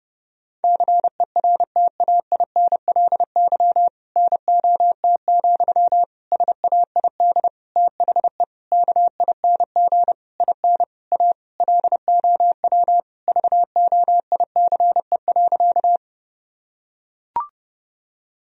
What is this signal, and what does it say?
Morse code, heard as a keyed tone, about 30 wpm, 700 Hz. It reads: CERTAINLY NOT, SAID THE KING IN A LOW VOICE.
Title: Sentences from Top 1000 Words - Review Part 1 - 30wpm